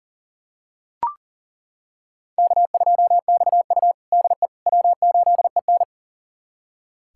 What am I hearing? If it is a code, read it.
Morse code, heard as a keyed tone, about 40 wpm, 700 Hz. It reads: K2XU DE W8EN
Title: USA Callsign DE Callsign - 4 Characters - 40wpm